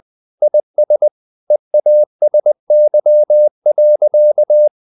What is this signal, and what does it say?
Morse code, heard as a keyed tone, about 20 wpm, 600 Hz. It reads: IS EASY.
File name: the old man and the sea part 2 20wpm